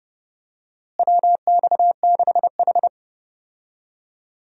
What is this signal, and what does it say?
Morse code, heard as a keyed tone, about 30 wpm, 700 Hz. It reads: WX6H
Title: USA Call Signs - 4 Characters - Repeated 2x - 30wpm